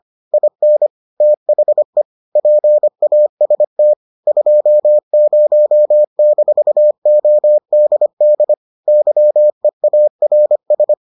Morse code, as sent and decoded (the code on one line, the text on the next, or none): .. -. / - .... / .--. .- ... - / ..--- ----- -....- --- -.. -.. / -.-- . .- .-. ...
IN THE PAST 20-ODD YEARS